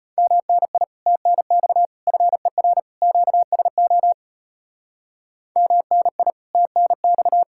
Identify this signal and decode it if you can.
Morse code, heard as a keyed tone, about 31 wpm, 700 Hz. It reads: MNI TNX FER QSO MNI TNX